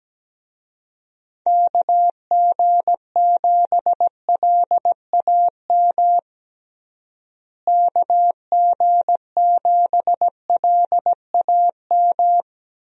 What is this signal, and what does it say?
Morse code, heard as a keyed tone, about 17 wpm, 700 Hz. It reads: KG7LAM KG7LAM
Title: USA Call Signs - 6 Characters - Repeated 3x - 17wpm